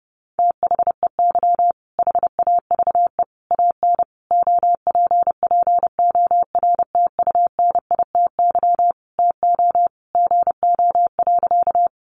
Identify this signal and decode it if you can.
Morse code, heard as a keyed tone, about 30 wpm, 700 Hz. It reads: THEY HAVE AN OPPORTUNITY TO GO.
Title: Sentences from Top 1000 Words 30wpm